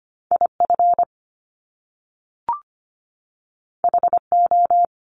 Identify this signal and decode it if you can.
Morse code, heard as a keyed tone, about 25 wpm, 700 Hz. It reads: IF HO
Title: Top Two Letter Combinations in Words 25wpm